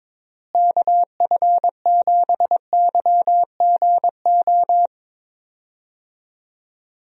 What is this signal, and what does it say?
Morse code, heard as a keyed tone, about 22 wpm, 700 Hz. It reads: KF7YGO
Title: USA Call Signs - 6 Characters - Repeated 2x - 22wpm